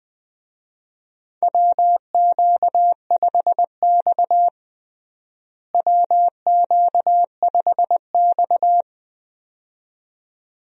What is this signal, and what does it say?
Morse code, heard as a keyed tone, about 20 wpm, 700 Hz. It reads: WQ5X WQ5X
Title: USA Call Signs - 4 Characters - Repeated 2x - 20wpm